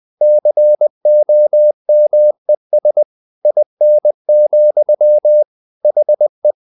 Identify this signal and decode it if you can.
Morse code, heard as a keyed tone, about 20 wpm, 600 Hz. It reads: COMES IN, HE
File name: the old man and the sea part 2 20wpm